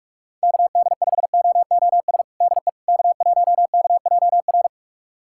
Morse code, heard as a keyed tone, about 41 wpm, 700 Hz. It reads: KD5OOS DE K1KJR
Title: USA Callsign DE Callsign 45wpm